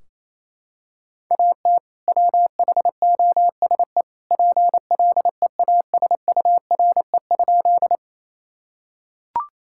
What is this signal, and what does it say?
Morse code, heard as a keyed tone, about 28 wpm, 700 Hz. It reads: AT WHOSE PLEASURE?
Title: Sentences from Top 1600 Words 28wpm